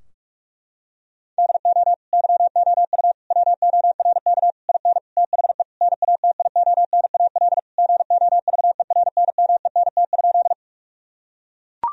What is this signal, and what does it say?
Morse code, heard as a keyed tone, about 42 wpm, 700 Hz. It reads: DO YOU WORK IN THE NATIONAL GOVERNMENT?
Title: Sentences from Top 1000 Words - Review Part 4 - 45wpm